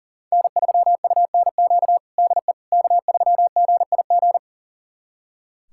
Morse code, heard as a keyed tone, about 40 wpm, 700 Hz. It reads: N3UNQ DE K3GIG
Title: USA Callsign DE Callsign - 5 Characters - 40wpm